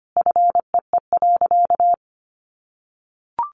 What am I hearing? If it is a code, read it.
Morse code, heard as a keyed tone, about 25 wpm, 700 Hz. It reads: FEE.
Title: Sentences from Top 1500 Words 25wpm